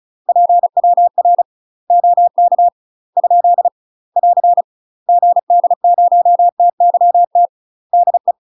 Morse code, heard as a keyed tone, about 35 wpm, 700 Hz. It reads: PWR OK ? <AR> GD0TYT DE
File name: UK MASTER QSOs TRAINING / 35WPM / UK MASTER QSOs 0043